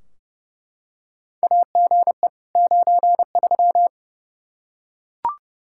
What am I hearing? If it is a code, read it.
Morse code, heard as a keyed tone, about 32 wpm, 700 Hz. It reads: AGE 93
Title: Practice QSO - Exchange 3 - More Info - 30wpm